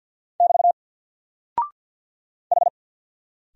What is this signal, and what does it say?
Morse code, heard as a keyed tone, about 48 wpm, 700 Hz. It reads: <BT> H
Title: Single Character (Letter:Number:Punctuation:Prosign) 50wpm